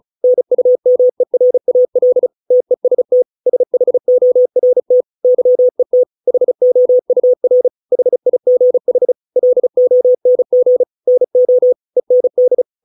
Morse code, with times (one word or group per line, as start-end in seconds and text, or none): 0.2-2.3 NUMERAL
2.5-3.2 TEST
3.5-5.0 SHORT
5.2-6.0 YET
6.3-7.7 HOUR
7.9-9.1 HIGH
9.4-10.8 LONG
11.1-11.7 NO
12.0-12.6 END